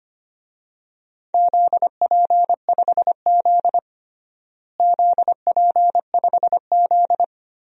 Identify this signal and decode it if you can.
Morse code, heard as a keyed tone, about 25 wpm, 700 Hz. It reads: ZP5Z ZP5Z
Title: DXCC-100 Call Signs - 4 Characters - Repeated 3x - 25wpm